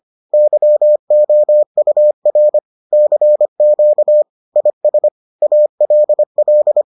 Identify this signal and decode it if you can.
Morse code, heard as a keyed tone, about 25 wpm, 600 Hz. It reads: YOUR CQ IS ALL